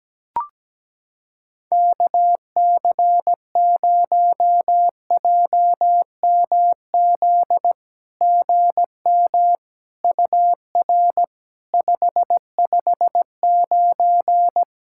Morse code, heard as a keyed tone, about 17 wpm, 700 Hz. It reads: KC0JMZ GM UR 559
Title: SOTA QSO Practice 17wpm